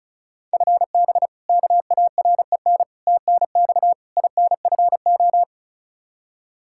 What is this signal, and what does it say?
Morse code, heard as a keyed tone, about 34 wpm, 700 Hz. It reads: FB KAREN TNX INFO
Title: K1USN SST QSO Practice 35wpm